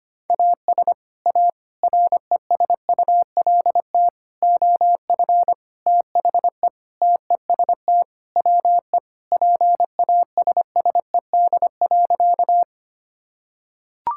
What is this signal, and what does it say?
Morse code, heard as a keyed tone, about 25 wpm, 700 Hz. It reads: AS A RESULT OF THE TEST WE PASSED.
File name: practice / 25wpm / Sentences with 4-Word Phrases - 25wpm